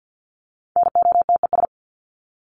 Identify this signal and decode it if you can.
Morse code, heard as a keyed tone, about 50 wpm, 700 Hz. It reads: NOTES
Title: ES:S Suffix - Top 100 Words with Base Words 50wpm